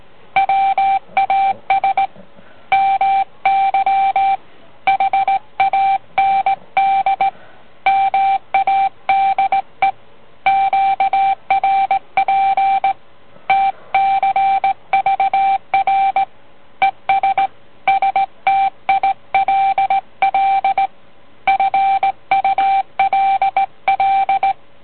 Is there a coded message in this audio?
Yes: Morse code from 0.4 to 24.5 s, heard as a keyed tone, about 18 wpm, 780 Hz. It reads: WAS MY HAND MADE QRP TCVR ES STILL FULL